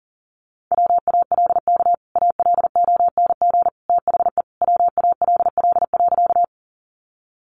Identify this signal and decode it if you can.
Morse code, heard as a keyed tone, about 40 wpm, 700 Hz. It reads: WALK ALONG THE WALL.
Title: Sentences from Top 1100 Words 40wpm